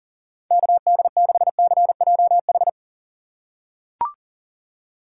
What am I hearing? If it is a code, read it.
Morse code, heard as a keyed tone, about 40 wpm, 700 Hz. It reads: KD6CJH
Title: USA Call Signs - 6 Characters - 4x Character Spacing - 40wpm